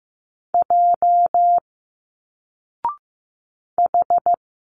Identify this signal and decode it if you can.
Morse code, heard as a keyed tone, about 15 wpm, 700 Hz. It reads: J H
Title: Single Letters 15wpm